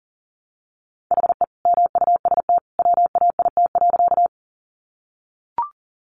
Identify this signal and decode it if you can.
Morse code, heard as a keyed tone, about 40 wpm, 700 Hz. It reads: HE MUST WAIT.
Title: Sentences from Top 500 Words 40wpm